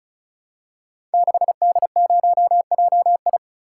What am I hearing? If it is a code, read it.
Morse code, heard as a keyed tone, about 35 wpm, 700 Hz. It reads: 6D0JI
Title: Code Groups - 5 Letter:Numbers - 35wpm